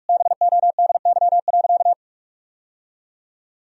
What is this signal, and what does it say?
Morse code, heard as a keyed tone, about 46 wpm, 700 Hz. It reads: BODY.
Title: Sentences from Top 1000 Words - Review Part 4 - 45wpm